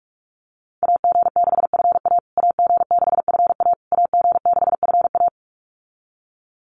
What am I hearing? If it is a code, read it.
Morse code, heard as a keyed tone, about 46 wpm, 700 Hz. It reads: AG6FA AG6FA AG6FA
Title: USA Call Signs Repeated 3x 45wpm